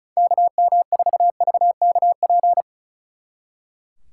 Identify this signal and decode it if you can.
Morse code, heard as a keyed tone, about 45 wpm, 700 Hz. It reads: KM4VKP